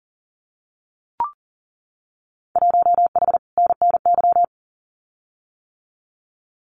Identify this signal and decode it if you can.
Morse code, heard as a keyed tone, about 40 wpm, 700 Hz. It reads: 1H NNY
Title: Winter Field Day Exchanges 40wpm